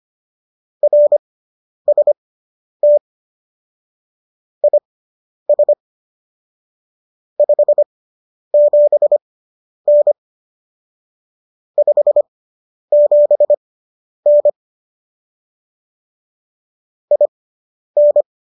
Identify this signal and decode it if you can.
Morse code, heard as a keyed tone, about 25 wpm, spaced as for 10 wpm, 590 Hz. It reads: RST IS 57N 57N IN